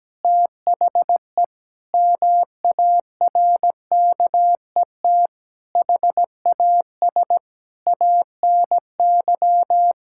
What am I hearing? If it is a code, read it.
Morse code, heard as a keyed tone, about 17 wpm, 700 Hz. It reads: THE MARKET HAS ANY
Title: Sentences from Top 1000 Words - Review Part 6 - 17wpm